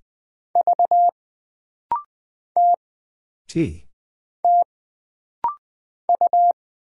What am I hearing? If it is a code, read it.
Morse code, heard as a keyed tone, about 20 wpm, 700 Hz. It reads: V T T U
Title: Single Character (Letter:Number:Punctuation:Prosign) - Rapid-Fire - 20wpm